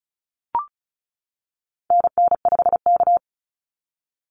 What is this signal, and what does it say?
Morse code, heard as a keyed tone, about 34 wpm, 700 Hz. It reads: NN5K
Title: NA SOTA Activator Callsigns 35wpm